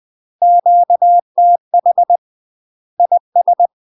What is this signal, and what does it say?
Morse code, heard as a keyed tone, about 20 wpm, 700 Hz. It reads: QTH IS
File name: UK MASTER QSOs TRAINING / 20WPM / UK MASTER QSOs 0041